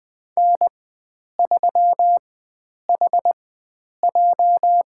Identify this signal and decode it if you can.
Morse code, heard as a keyed tone, about 20 wpm, spaced as for 9 wpm, 700 Hz. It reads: N3HJ
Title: USA Call Signs - 4x Character Spacing - 20wpm